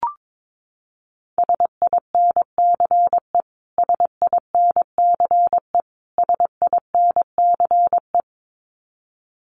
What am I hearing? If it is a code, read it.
Morse code, heard as a keyed tone, about 22 wpm, 700 Hz. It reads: SINCE SINCE SINCE
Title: Top 300 Words Repeated 22wpm